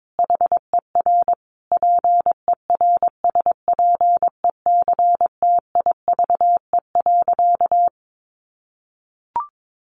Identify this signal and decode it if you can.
Morse code, heard as a keyed tone, about 22 wpm, 700 Hz. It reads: HER PERSPECTIVE.
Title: Sentences from Top 1200 Words 22wpm